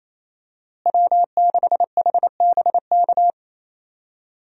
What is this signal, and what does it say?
Morse code, heard as a keyed tone, about 28 wpm, 700 Hz. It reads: W6HBK
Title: USA Call Signs - 5 Characters - Repeated 3x - 28wpm